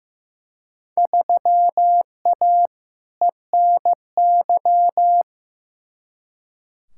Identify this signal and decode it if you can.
Morse code, heard as a keyed tone, about 15 wpm, 700 Hz. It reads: 3A ENY
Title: Field Day Exchanges - 15wpm